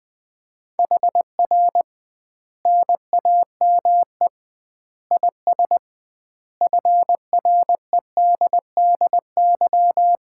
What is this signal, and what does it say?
Morse code, heard as a keyed tone, about 20 wpm, 700 Hz. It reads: HR NAME IS FREDDY
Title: Common QSO Phrases - 2x Word-Spacing - 20wpm